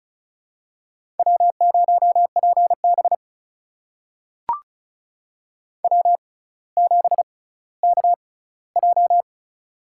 Morse code, heard as a keyed tone, about 34 wpm, spaced as for 18 wpm, 700 Hz. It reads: W0PB W7KJ
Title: USA Call Signs - 6x Character Spacing - 35wpm